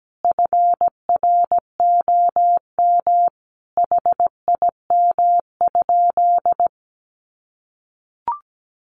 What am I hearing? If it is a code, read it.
Morse code, heard as a keyed tone, about 17 wpm, 700 Hz. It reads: FROM HIM?
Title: Sentences from Top 1300 Words 17wpm